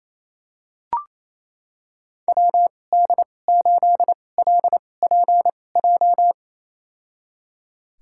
Morse code, heard as a keyed tone, about 28 wpm, spaced as for 20 wpm, 700 Hz. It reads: WD8LPJ
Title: USA Call Signs - 6 Characters - 2x Character Spacing - 28wpm